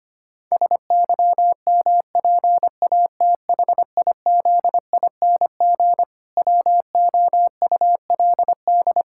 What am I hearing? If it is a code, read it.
Morse code, heard as a keyed tone, about 25 wpm, 700 Hz. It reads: SYMPATHIZING WOULD
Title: Pangram Challenge v2 - 25wpm